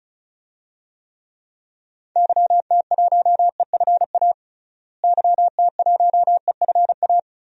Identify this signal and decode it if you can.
Morse code, heard as a keyed tone, about 34 wpm, 700 Hz. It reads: YT1EFA YT1EFA
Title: DXCC-100 Call Signs - 6 Characters - Repeated 2x - 35wpm